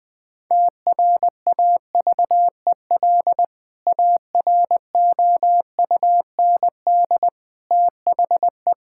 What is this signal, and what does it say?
Morse code, heard as a keyed tone, about 21 wpm, 700 Hz. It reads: TRAVEL AROUND THE
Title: Sentences with 3-Word Phrases - Spoken Beginning - 20wpm